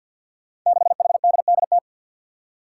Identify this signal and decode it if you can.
Morse code, heard as a keyed tone, about 53 wpm, 700 Hz. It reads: 6HDDT
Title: Five Challenging Letter-Number Combinations 50wpm